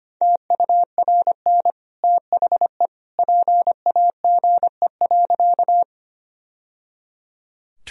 Morse code, heard as a keyed tone, about 25 wpm, 700 Hz. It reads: TURN THE PAGE.
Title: Sentences from Top 1000 Words - Review Part 1 - 25wpm